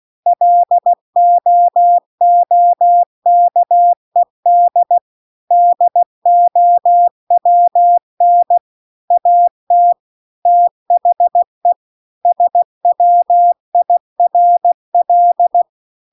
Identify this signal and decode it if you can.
Morse code, heard as a keyed tone, about 16 wpm, 700 Hz. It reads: LOOKED DOWN AT THE SWIRL